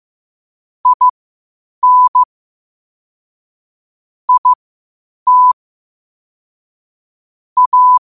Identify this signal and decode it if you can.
Morse code, heard as a keyed tone, about 15 wpm, spaced as for 8 wpm, 1000 Hz. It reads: IN IT A
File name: qotdfiles / 8wpm